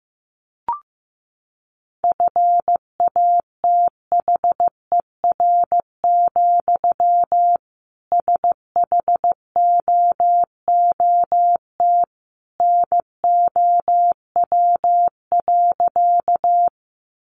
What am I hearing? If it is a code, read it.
Morse code, heard as a keyed tone, about 15 wpm, 700 Hz. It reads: FATHER, SHOOT NOW.
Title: Sentences from Top 1200 Words 15wpm